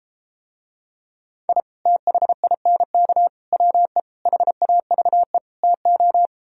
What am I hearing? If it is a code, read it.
Morse code, heard as a keyed tone, about 33 wpm, 700 Hz. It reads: I THINK WE HAVE TO